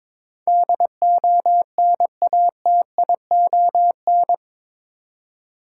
Morse code, heard as a keyed tone, about 22 wpm, 700 Hz. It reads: DONATION
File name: practice / 22wpm / Eight Letter Words - 22wpm